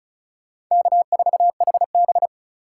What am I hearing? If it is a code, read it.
Morse code, heard as a keyed tone, about 45 wpm, 700 Hz. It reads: K4HB